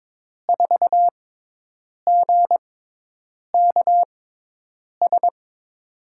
Morse code, heard as a keyed tone, about 22 wpm, spaced as for 8 wpm, 700 Hz. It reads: 4GKS
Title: USA Call Signs - 6x Character Spacing - 22wpm